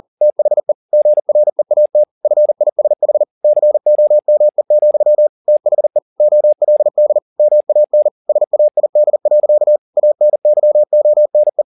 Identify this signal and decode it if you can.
Morse code, heard as a keyed tone, about 40 wpm, 600 Hz. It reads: THE GREAT FISH COME, THE OLD MAN SAID. ANYONE